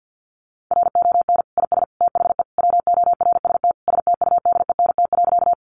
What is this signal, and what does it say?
Morse code, heard as a keyed tone, about 48 wpm, 700 Hz. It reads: RON IS THE WORST STUDENT.
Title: Additional Easy Phrases 50wpm